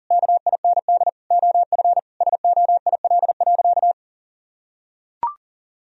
Morse code, heard as a keyed tone, about 40 wpm, 700 Hz. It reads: KIND OF SOIL.
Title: Sentences from Top 1000 Words - Review Part 4 - 40wpm